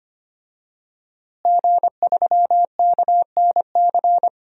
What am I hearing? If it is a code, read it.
Morse code, heard as a keyed tone, about 25 wpm, 700 Hz. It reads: G3KNC
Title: DXCC-100 Call Signs - 5 Characters - Repeated 3x - 25wpm